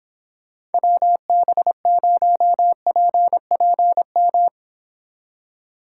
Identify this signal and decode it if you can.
Morse code, heard as a keyed tone, about 29 wpm, 700 Hz. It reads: WB0PPM